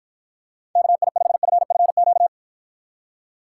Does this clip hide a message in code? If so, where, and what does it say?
Yes: Morse code from 0.8 to 2.3 s, heard as a keyed tone, about 54 wpm, 700 Hz. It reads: KI5FFQ